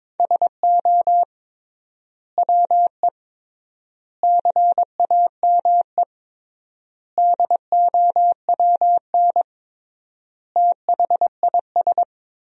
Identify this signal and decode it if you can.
Morse code, heard as a keyed tone, about 22 wpm, 700 Hz. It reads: SO WE CAME DOWN THIS
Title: 3x Word-Spacing - Sentences from Top 400 Words - 22wpm